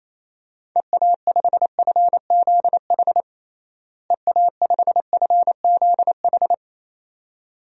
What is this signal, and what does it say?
Morse code, heard as a keyed tone, about 27 wpm, 700 Hz. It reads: EA5FZH EA5FZH
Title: DXCC-100 Call Signs - 6 Characters - Repeated 2x - 28wpm